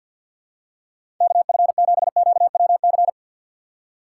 Morse code, heard as a keyed tone, about 53 wpm, 700 Hz. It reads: KF7QWC
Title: USA Call Signs - 6 Characters - 4x Character Spacing - 50wpm